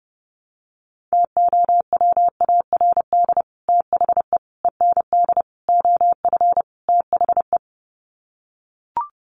Morse code, heard as a keyed tone, about 30 wpm, 700 Hz. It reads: TOWARD THE END OF THE
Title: Sets of 5 Words 30wpm